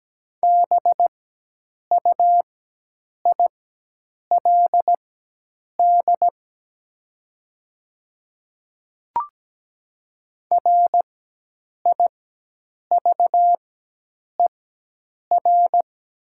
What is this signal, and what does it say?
Morse code, heard as a keyed tone, about 17 wpm, spaced as for 8 wpm, 690 Hz. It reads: BUILD RIVER
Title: 4x Character Spacing - Top 300 Words - 17wpm